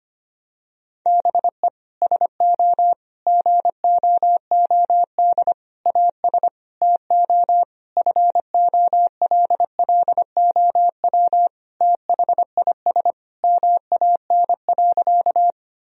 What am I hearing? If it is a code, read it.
Morse code, heard as a keyed tone, about 25 wpm, 700 Hz. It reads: BE SO GOOD AS TO FOLLOW THIS MAN.
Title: Sentences from Top 1000 Words - Review Part 1 - 25wpm